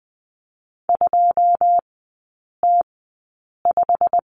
Morse code, heard as a keyed tone, about 20 wpm, spaced as for 8 wpm, 700 Hz. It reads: 2T5
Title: 121 Lesson 20e - Number 3 - Three Characters 20wpm